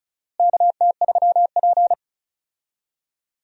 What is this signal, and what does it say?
Morse code, heard as a keyed tone, about 36 wpm, 700 Hz. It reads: KT3P
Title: USA Call Signs - 4 Characters - Repeated 2x - 35wpm